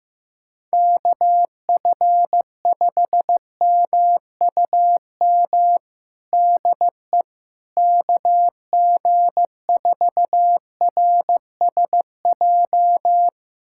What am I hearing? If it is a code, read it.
Morse code, heard as a keyed tone, about 16 wpm, 700 Hz. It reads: KF5MUM DE KG4RSJ
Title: USA Callsign DE Callsign - 6 Characters - 15wpm